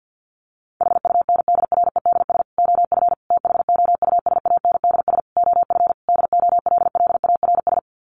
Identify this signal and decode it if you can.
Morse code, heard as a keyed tone, about 50 wpm, 700 Hz. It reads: HUNDREDS OF THOUSANDS OF DOLLARS
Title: Sets of 5 Words Encore 50wpm